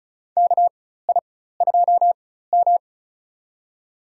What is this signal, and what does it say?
Morse code, heard as a keyed tone, about 34 wpm, spaced as for 16 wpm, 700 Hz. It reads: KI2M
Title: USA Call Signs - 4x Character Spacing - 35wpm